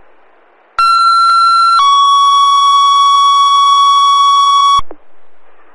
Two-tone page, heard as a one-second tone, then a three-second tone, about 1360 then 1120 Hz. It is heard from 0.8 to 4.8 s.
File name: fire / rowe